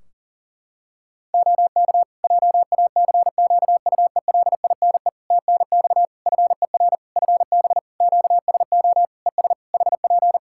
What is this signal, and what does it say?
Morse code, heard as a keyed tone, about 40 wpm, 700 Hz. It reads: OK JACQUELINE TNX FER FB QSO ES HP
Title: K1USN SST QSO Practice 40wpm